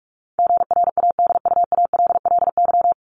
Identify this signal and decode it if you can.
Morse code, heard as a keyed tone, about 44 wpm, 700 Hz. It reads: GRADUALLY
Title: LY Suffix - Top 100 Words with Base Words 45wpm